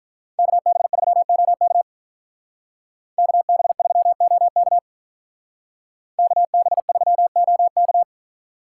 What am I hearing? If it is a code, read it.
Morse code, heard as a keyed tone, about 43 wpm, 700 Hz. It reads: KB3OK KB3OK KB3OK
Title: Speed Racing - USA 2x2 Call Signs 35wpm